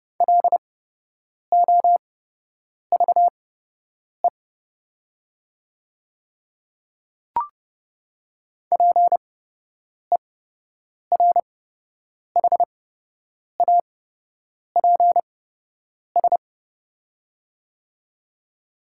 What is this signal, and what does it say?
Morse code, heard as a keyed tone, about 30 wpm, spaced as for 8 wpm, 700 Hz. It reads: LOVE PERHAPS
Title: 8x Character Spacing - Top 500 Words - 30wpm